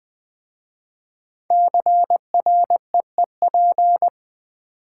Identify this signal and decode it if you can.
Morse code, heard as a keyed tone, about 20 wpm, 700 Hz. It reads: CREEP
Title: 5 Letter Words - First 3 Letters Given - 20wpm